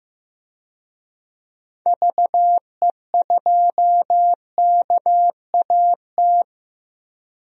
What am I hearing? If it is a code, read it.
Morse code, heard as a keyed tone, about 15 wpm, 700 Hz. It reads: VE2KAT
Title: DXCC-100 Call Signs - 6 Characters - Repeated 2x - 15wpm